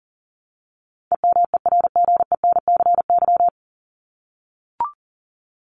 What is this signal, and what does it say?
Morse code, heard as a keyed tone, about 40 wpm, 700 Hz. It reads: EMERGENCY
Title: Top 1501-1750 Words 40wpm